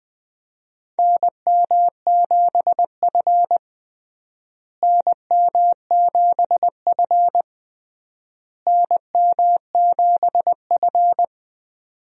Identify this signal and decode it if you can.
Morse code, heard as a keyed tone, about 20 wpm, 700 Hz. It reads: NM7F NM7F NM7F
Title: USA Call Signs - 4 Characters - Repeated 3x - 20wpm